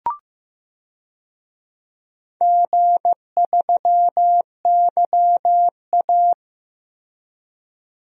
Morse code, heard as a keyed tone, about 15 wpm, 700 Hz. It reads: G3YA
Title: DXCC-100 Call Signs - 4 Characters - Repeated 3x - 15wpm